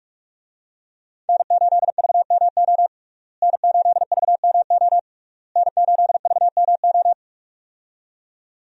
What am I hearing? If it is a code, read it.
Morse code, heard as a keyed tone, about 44 wpm, 700 Hz. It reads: N8VMO N8VMO N8VMO
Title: USA Call Signs - 5 Characters - Repeated 3x - 45wpm